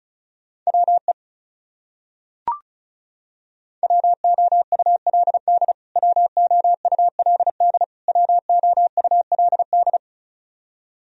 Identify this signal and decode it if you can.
Morse code, heard as a keyed tone, about 36 wpm, 700 Hz. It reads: WE WOULD WOULD WOULD
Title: Top 100 Words Repeated 35wpm